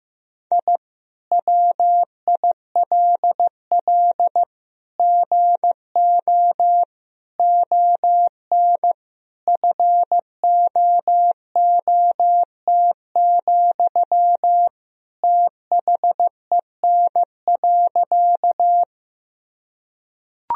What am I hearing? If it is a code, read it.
Morse code, heard as a keyed tone, about 15 wpm, 700 Hz. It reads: I WILL GO ON FOOT, THEN.
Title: Sentences from Top 1000 Words - Review Part 3 - 15wpm